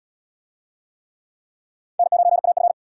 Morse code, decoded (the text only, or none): N9AF